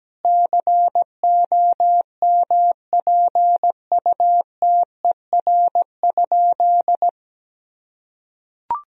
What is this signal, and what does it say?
Morse code, heard as a keyed tone, about 17 wpm, 700 Hz. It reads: COMPUTER?
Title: Sentences from Top 1000 Words - Review Part 3 - 17wpm